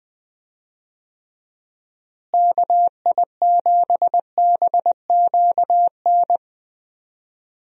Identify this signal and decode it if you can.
Morse code, heard as a keyed tone, about 20 wpm, 700 Hz. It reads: KI7BQN
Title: USA Call Signs - 6 Characters - Repeated 3x - 20wpm